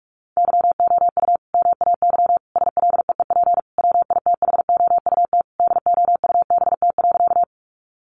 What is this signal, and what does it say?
Morse code, heard as a keyed tone, about 44 wpm, 700 Hz. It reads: YOU MAY SLEEP WITHOUT DOUBT.
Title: Sentences from Top 1500 Words 45wpm